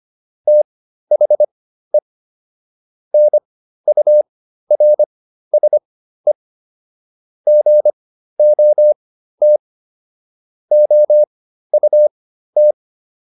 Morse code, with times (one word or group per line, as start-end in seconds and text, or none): 0.5-2.0 THE
3.1-6.3 NURSE
7.5-9.6 GOT
10.7-12.7 OUT